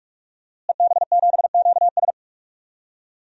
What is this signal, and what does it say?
Morse code, heard as a keyed tone, about 44 wpm, 700 Hz. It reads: EB7QS
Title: DXCC-100 Call Signs - 5 Characters - Repeated 3x - 45wpm